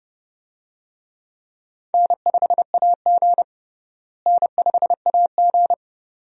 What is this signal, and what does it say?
Morse code, heard as a keyed tone, about 32 wpm, 700 Hz. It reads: N5AG N5AG
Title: USA Call Signs - 4 Characters - Repeated 2x - 30wpm